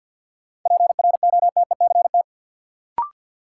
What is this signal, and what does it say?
Morse code, heard as a keyed tone, about 49 wpm, 700 Hz. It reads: PROTECT
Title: Sentences from Top 900 Words - New Word Drill - 50wpm